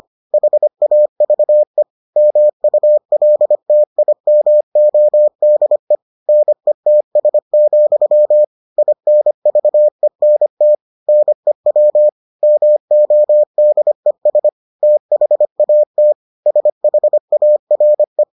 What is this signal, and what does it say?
Morse code, heard as a keyed tone, about 25 wpm, 600 Hz. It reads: HAVE MULTIMODE NETS, INVENT NEW MODES THAT SHARE